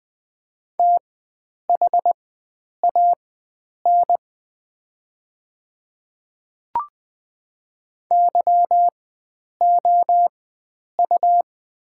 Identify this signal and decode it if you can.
Morse code, heard as a keyed tone, about 20 wpm, spaced as for 9 wpm, 700 Hz. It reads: THAN YOU
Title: 4x Character Spacing - Top 300 Words - 20wpm